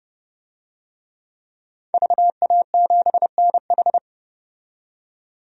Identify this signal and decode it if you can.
Morse code, heard as a keyed tone, about 30 wpm, 700 Hz. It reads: VA7NH
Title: DXCC-100 Call Signs - 5 Characters - 30wpm